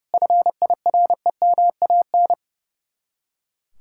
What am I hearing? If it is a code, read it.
Morse code, heard as a keyed tone, about 38 wpm, 700 Hz. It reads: FIREMAN